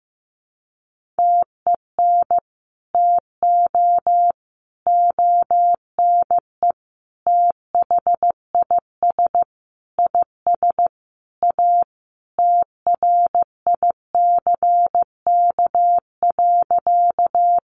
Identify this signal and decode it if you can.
Morse code, heard as a keyed tone, about 15 wpm, 700 Hz. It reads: TEN TO ONE THIS IS A TRICK.
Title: Sentences from Top 1500 Words 15wpm